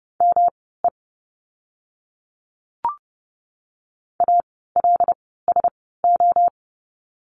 Morse code, heard as a keyed tone, about 30 wpm, spaced as for 17 wpm, 700 Hz. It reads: ME ALSO
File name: course / 30:10wpm / 230 Lesson 37f - Letter x - Words 30wpm